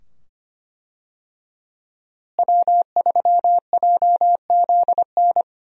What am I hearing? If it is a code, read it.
Morse code, heard as a keyed tone, about 25 wpm, 700 Hz. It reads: W3JZN
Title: USA Call Signs - 5 Characters - Repeated 2x - 25wpm